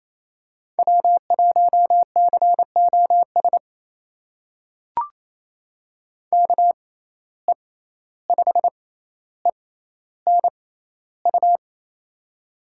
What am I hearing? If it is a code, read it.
Morse code, heard as a keyed tone, about 28 wpm, spaced as for 14 wpm, 700 Hz. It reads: W1COS KE5ENU
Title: USA Call Signs - 6x Character Spacing - 28wpm